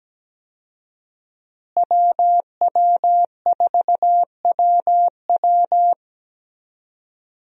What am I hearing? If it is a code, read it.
Morse code, heard as a keyed tone, about 17 wpm, 700 Hz. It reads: WW4WW
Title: USA Call Signs - 5 Characters - Repeated 2x - 17wpm